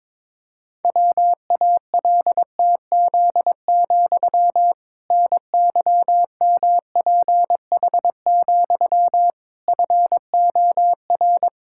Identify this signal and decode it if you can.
Morse code, heard as a keyed tone, about 22 wpm, 700 Hz. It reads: WALTZ, NYMPH, FOR
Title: Pangram Challenge v2 - 22wpm